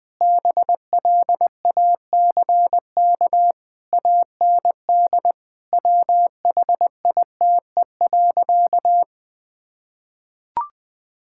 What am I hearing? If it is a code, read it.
Morse code, heard as a keyed tone, about 21 wpm, 700 Hz. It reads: BLACK AND WHITE.
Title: Sentences with 5-Word Phrases - 20wpm